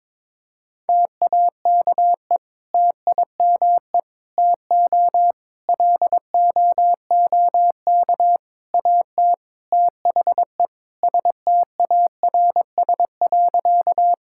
Morse code, heard as a keyed tone, about 22 wpm, 700 Hz. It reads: TAKE TIME TO LOOK AT THE STARS.
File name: practice / 22wpm / Sentences with 4-Word Phrases - 22wpm